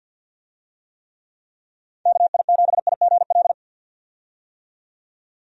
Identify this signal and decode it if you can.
Morse code, heard as a keyed tone, about 51 wpm, 700 Hz. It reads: KI7IGL
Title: USA Call Signs - 6 Characters - 50wpm